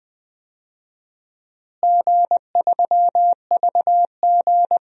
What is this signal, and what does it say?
Morse code, heard as a keyed tone, about 21 wpm, 700 Hz. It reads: G3VG
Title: DXCC-100 Call Signs - 4 Characters - Repeated 2x - 20wpm